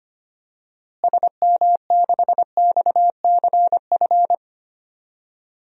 Morse code, heard as a keyed tone, about 25 wpm, 700 Hz. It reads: SM6XCF
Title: DXCC-100 Call Signs - 6 Characters - Repeated 3x - 25wpm